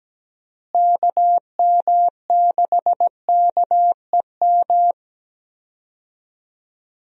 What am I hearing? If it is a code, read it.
Morse code, heard as a keyed tone, about 17 wpm, 700 Hz. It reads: KM6KEM
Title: USA Call Signs - 6 Characters - Repeated 2x - 17wpm